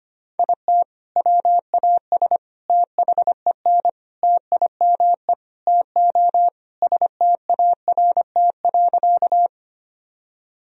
Morse code, heard as a keyed tone, about 25 wpm, 700 Hz. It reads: IT WAS THEN TIME TO START.
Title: Sentences from Top 1000 Words - Review Part 1 - 25wpm